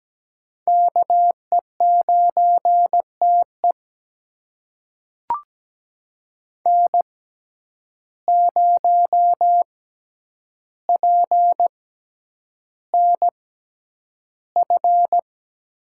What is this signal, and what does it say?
Morse code, heard as a keyed tone, about 17 wpm, spaced as for 9 wpm, 700 Hz. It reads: KE9TE N0PNF
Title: USA Call Signs - 5 Characters - 6x Character Spacing - 17wpm